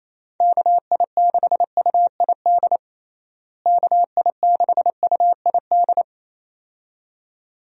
Morse code, heard as a keyed tone, about 28 wpm, 700 Hz. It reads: KI6UID KI6UID
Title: USA Call Signs - 6 Characters - Repeated 2x - 28wpm